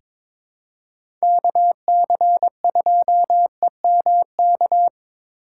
Morse code, heard as a keyed tone, about 23 wpm, 700 Hz. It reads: KC2EMK
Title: USA Call Signs - 3x Character Spacing - 22wpm